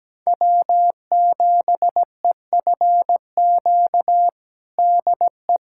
Morse code, heard as a keyed tone, about 17 wpm, 700 Hz. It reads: W7EFQ DE
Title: USA Callsign DE Callsign - 5 Characters - 17wpm